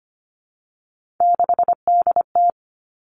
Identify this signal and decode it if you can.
Morse code, heard as a keyed tone, about 25 wpm, 700 Hz. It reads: 6DT
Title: Three Challenging Letter-Number Combinations 25wpm